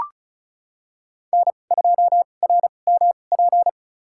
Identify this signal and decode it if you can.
Morse code, heard as a keyed tone, about 36 wpm, spaced as for 25 wpm, 700 Hz. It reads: N2RMP
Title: USA Call Signs - 5 Characters - 2x Character Spacing - 35wpm